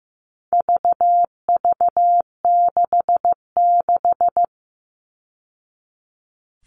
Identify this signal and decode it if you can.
Morse code, heard as a keyed tone, about 15 wpm, 700 Hz. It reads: VV66
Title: All Cut Numbers - 4-Digit 15wpm